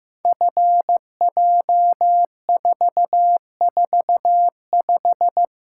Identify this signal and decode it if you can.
Morse code, heard as a keyed tone, about 15 wpm, 700 Hz. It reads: FJ445
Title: Code Groups - 5 Letter:Numbers - 15wpm